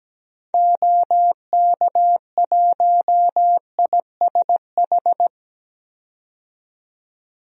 Morse code, heard as a keyed tone, about 17 wpm, 700 Hz. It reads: OK1ISH
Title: DXCC-100 Call Signs - 6 Characters - 17wpm